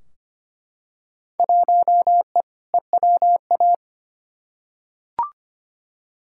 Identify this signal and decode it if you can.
Morse code, heard as a keyed tone, about 25 wpm, 700 Hz. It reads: 1E EWA
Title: Field Day Exchanges - 25wpm